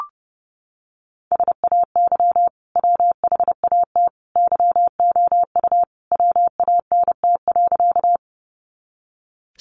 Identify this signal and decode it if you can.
Morse code, heard as a keyed tone, about 30 wpm, 700 Hz. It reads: SAY WHAT YOU WANT.
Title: Sentences from Top 200 Words 30wpm